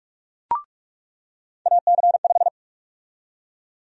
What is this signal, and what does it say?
Morse code, heard as a keyed tone, about 43 wpm, 700 Hz. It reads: AC5
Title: USA Call Sign Prefixes - 45wpm